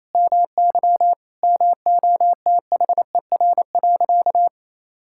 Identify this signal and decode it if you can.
Morse code, heard as a keyed tone, about 28 wpm, 700 Hz. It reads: MY MOTHER.
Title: Sentences from Top 1600 Words 28wpm